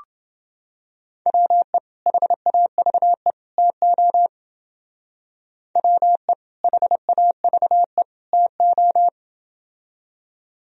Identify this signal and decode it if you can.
Morse code, heard as a keyed tone, about 29 wpm, 700 Hz. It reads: WE HAVE TO WE HAVE TO